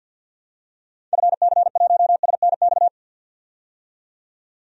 Morse code, heard as a keyed tone, about 49 wpm, 700 Hz. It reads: FC1SNX
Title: DXCC-100 Call Signs - 6 Characters - 50wpm